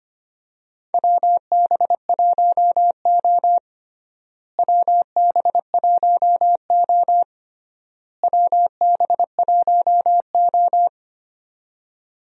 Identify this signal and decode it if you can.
Morse code, heard as a keyed tone, about 25 wpm, 700 Hz. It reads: WB1O WB1O WB1O
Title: USA Call Signs - 4 Characters - Repeated 3x - 25wpm